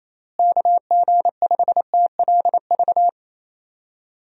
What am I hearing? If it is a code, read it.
Morse code, heard as a keyed tone, about 28 wpm, 700 Hz. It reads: KG5TLV
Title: USA Call Signs - 6 Characters - Repeated 3x - 28wpm